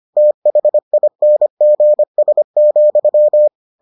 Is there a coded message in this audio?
Yes: Morse code from 0.2 to 3.5 s, heard as a keyed tone, about 25 wpm, 600 Hz. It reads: THINGS,